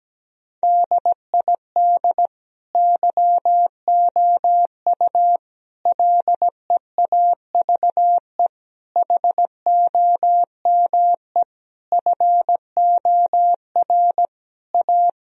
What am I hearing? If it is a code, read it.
Morse code, heard as a keyed tone, about 17 wpm, 700 Hz. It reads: DID YOU LEAVE HOME FOR A
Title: Sentences from Top 1000 Words - Review Part 4 - 17wpm